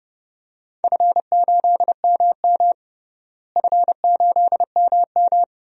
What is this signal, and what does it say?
Morse code, heard as a keyed tone, about 30 wpm, 700 Hz. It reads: F8MM F8MM
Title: DXCC-100 Call Signs - 4 Characters - Repeated 2x - 30wpm